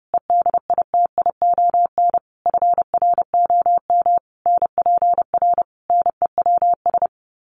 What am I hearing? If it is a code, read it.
Morse code, heard as a keyed tone, about 30 wpm, 700 Hz. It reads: EDITION FROM NPR NEWS
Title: Sets of 5 Words 30wpm